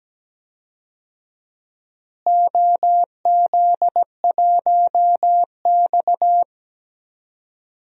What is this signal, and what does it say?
Morse code, heard as a keyed tone, about 17 wpm, 700 Hz. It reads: OZ1X
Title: DXCC-100 Call Signs - 4 Characters - Repeated 3x - 17wpm